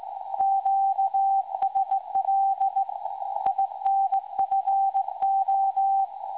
Morse code, heard as a keyed tone, about 18 wpm, 780 Hz. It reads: QSL INFO
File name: WebSDR records / SAQ recording 17.2kHz